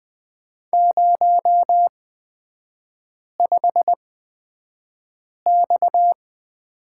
Morse code, heard as a keyed tone, about 20 wpm, spaced as for 5 wpm, 700 Hz. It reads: 05X
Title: 229 Lesson 37e - Letter x - Three Characters 20wpm